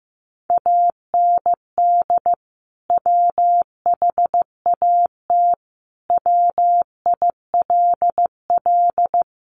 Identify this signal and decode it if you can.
Morse code, heard as a keyed tone, about 15 wpm, 700 Hz. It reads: AND WHAT WILL
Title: Sentences from Top 600 Words 15wpm